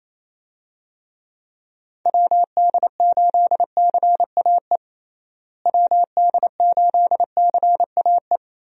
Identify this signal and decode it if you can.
Morse code, heard as a keyed tone, about 28 wpm, 700 Hz. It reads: WD8CAE WD8CAE
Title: USA Call Signs - 6 Characters - Repeated 3x - 28wpm